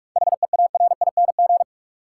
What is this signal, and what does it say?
Morse code, heard as a keyed tone, about 46 wpm, 700 Hz. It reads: HEARING
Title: Seven Letter Words - 45wpm